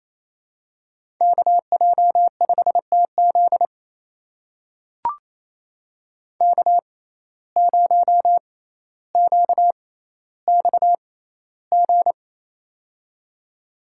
Morse code, heard as a keyed tone, about 28 wpm, spaced as for 14 wpm, 700 Hz. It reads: KJ5TZ K0QXG
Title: USA Call Signs - 5 Characters - 6x Character Spacing - 28wpm